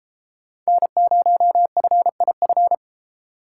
Morse code, heard as a keyed tone, about 30 wpm, 700 Hz. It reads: N0FIF